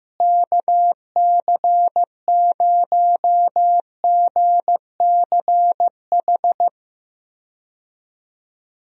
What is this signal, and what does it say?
Morse code, heard as a keyed tone, about 15 wpm, 700 Hz. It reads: KC0GCH
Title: USA Call Signs - 6 Characters - Repeated 3x - 15wpm